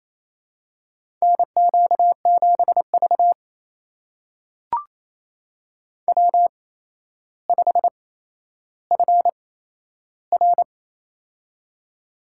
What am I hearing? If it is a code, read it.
Morse code, heard as a keyed tone, about 27 wpm, spaced as for 12 wpm, 700 Hz. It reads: NQ7V W5FR
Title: USA Call Signs - 4 Characters - 8x Character Spacing - 28wpm